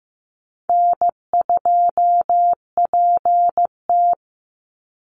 Morse code, heard as a keyed tone, about 15 wpm, 700 Hz. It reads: N2PT
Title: USA Call Signs 15wpm